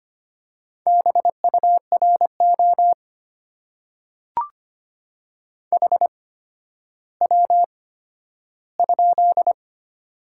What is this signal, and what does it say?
Morse code, heard as a keyed tone, about 25 wpm, spaced as for 12 wpm, 700 Hz. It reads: BURO HW?
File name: Top 90 QSO Elements - 8x Character Spacing - 25wpm